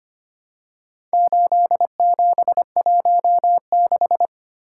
Morse code, heard as a keyed tone, about 25 wpm, 700 Hz. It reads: 8716
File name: practice / 25wpm / Four Letter-Number Combinations - 25wpm